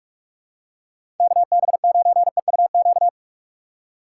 Morse code, heard as a keyed tone, about 42 wpm, 700 Hz. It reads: KB9EUQ
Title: USA Call Signs - 6 Characters - 6x Character Spacing - 45wpm